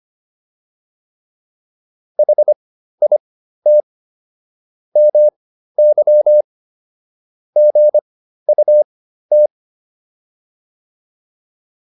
Morse code, heard as a keyed tone, about 25 wpm, spaced as for 13 wpm, 600 Hz. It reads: HIT MY GUT